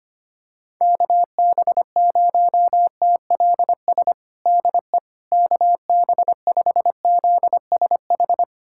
Morse code, heard as a keyed tone, about 25 wpm, 700 Hz. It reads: KB0TLS DE KB5ZSH
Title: USA Callsign DE Callsign - 6 Characters - 25wpm